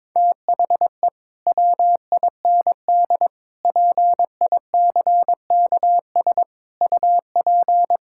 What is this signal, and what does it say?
Morse code, heard as a keyed tone, about 22 wpm, 700 Hz. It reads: THE WIND PICKS UP
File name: Difficult Sentences with Hints - 22wpm